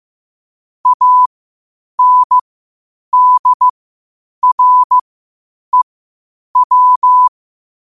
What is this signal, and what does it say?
Morse code, heard as a keyed tone, about 15 wpm, spaced as for 8 wpm, 1000 Hz. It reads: ANDREW
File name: qotdfiles / 8wpm